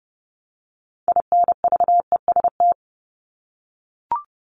Morse code, heard as a keyed tone, about 30 wpm, 700 Hz. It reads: INVEST
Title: Top 1001-1250 Words 30wpm